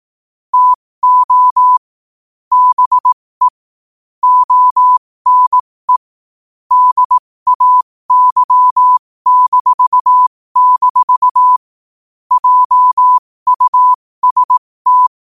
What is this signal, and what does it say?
Morse code, heard as a keyed tone, about 18 wpm, spaced as for 15 wpm, 1000 Hz. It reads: TO BE ONE DAY-- JUST
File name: qotdfiles / 15wpm